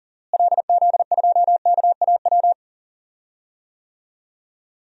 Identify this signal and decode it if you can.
Morse code, heard as a keyed tone, about 40 wpm, 700 Hz. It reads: LZ2KAW
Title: DXCC-100 Call Signs - 6 Characters - Repeated 2x - 40wpm